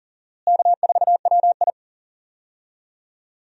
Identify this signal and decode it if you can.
Morse code, heard as a keyed tone, about 40 wpm, 700 Hz. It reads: K4WI
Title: USA Call Signs - 4 Characters - Repeated 2x - 40wpm